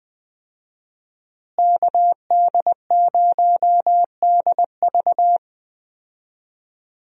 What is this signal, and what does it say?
Morse code, heard as a keyed tone, about 21 wpm, 700 Hz. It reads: KD0DV
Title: USA Call Signs - 5 Characters - Repeated 3x - 20wpm